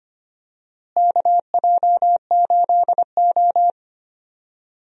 Morse code, heard as a keyed tone, about 25 wpm, 700 Hz. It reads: KJ8O
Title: USA Call Signs - 4 Characters - Repeated 2x - 25wpm